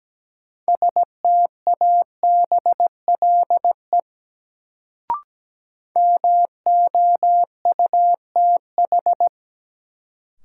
Morse code, heard as a keyed tone, about 17 wpm, 700 Hz. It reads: STABLE MOUTH
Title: Sentences from Top 1500 Words - New Word Drill - 17wpm